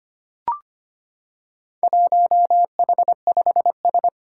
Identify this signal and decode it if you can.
Morse code, heard as a keyed tone, about 25 wpm, 700 Hz. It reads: 1H5S
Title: Four Challenging Letter-Number Combinations 25wpm